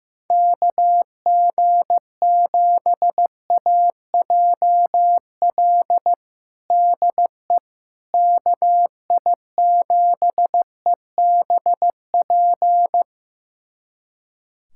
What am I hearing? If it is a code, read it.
Morse code, heard as a keyed tone, about 15 wpm, 700 Hz. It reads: KG7AJL DE KI7EBP
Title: USA Callsign DE Callsign - 6 Characters - 15wpm